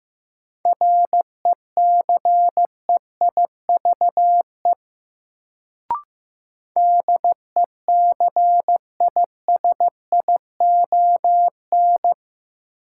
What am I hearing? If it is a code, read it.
Morse code, heard as a keyed tone, about 15 wpm, 700 Hz. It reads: RECEIVE DECISION
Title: Sentences from Top 1000 Words - New Word Drill - 15wpm